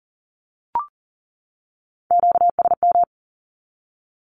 Morse code, heard as a keyed tone, about 40 wpm, 700 Hz. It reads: QSM
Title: Q-Codes 40wpm